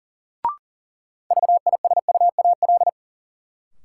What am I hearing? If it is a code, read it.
Morse code, heard as a keyed tone, about 39 wpm, 700 Hz. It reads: VISUAL
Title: Sentences from Top 1000 Words - Review Part 6 - New Word Drill - 40wpm